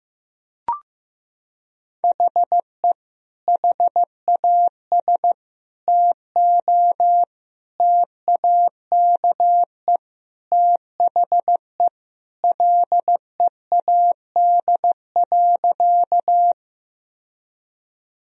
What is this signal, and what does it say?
Morse code, heard as a keyed tone, about 15 wpm, 700 Hz. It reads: HE HAS TO TAKE THE LEAD.
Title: Sentences from Top 1000 Words - Review Part 2 - 15wpm